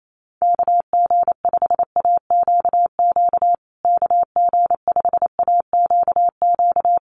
Morse code, heard as a keyed tone, about 28 wpm, 700 Hz. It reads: KG5AQQ KG5AQQ
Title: USA Call Signs Repeated 3x 28wpm